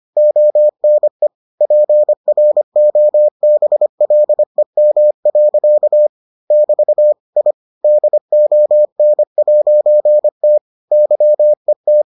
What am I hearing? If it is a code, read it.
Morse code, heard as a keyed tone, about 25 wpm, 600 Hz. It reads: ONE PROBLEM. <BT> I DON'T YET